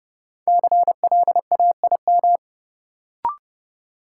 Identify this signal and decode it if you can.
Morse code, heard as a keyed tone, about 30 wpm, 700 Hz. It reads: CLAIM
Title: Sentences from Top 1500 Words - New Word Drill - 30wpm